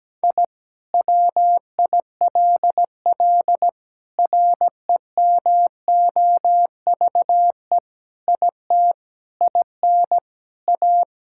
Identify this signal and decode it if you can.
Morse code, heard as a keyed tone, about 17 wpm, 700 Hz. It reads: I WILL REMOVE IT IN A
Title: Sentences from Top 1000 Words - Review Part 5 - 17wpm